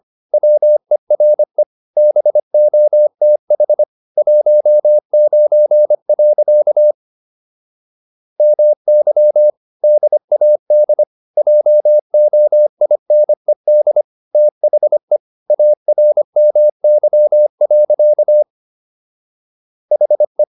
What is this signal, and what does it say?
Morse code, heard as a keyed tone, about 25 wpm, 600 Hz. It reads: WERE BOTH 19. MY DAD JOINED THE ARMY. HE